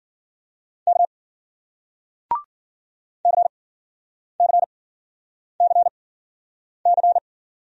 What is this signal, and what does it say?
Morse code, heard as a keyed tone, about 52 wpm, spaced as for 9 wpm, 700 Hz. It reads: K CCCC